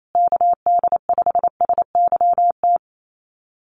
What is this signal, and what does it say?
Morse code, heard as a keyed tone, about 28 wpm, 700 Hz. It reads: KD5SYT
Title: USA Call Signs Repeated 3x 28wpm